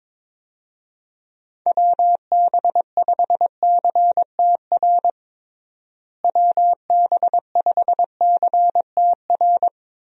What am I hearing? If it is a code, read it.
Morse code, heard as a keyed tone, about 21 wpm, 700 Hz. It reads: WB5CTR WB5CTR
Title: USA Call Signs - 6 Characters - Repeated 2x - 22wpm